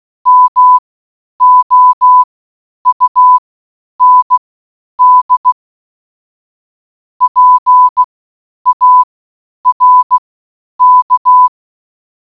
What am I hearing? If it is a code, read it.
Morse code, heard as a keyed tone, about 16 wpm, spaced as for 9 wpm, 1000 Hz. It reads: MOUND PARK